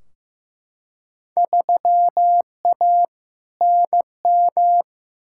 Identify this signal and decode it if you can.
Morse code, heard as a keyed tone, about 15 wpm, 700 Hz. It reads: 3A NM
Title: Field Day Exchanges Encore - 15wpm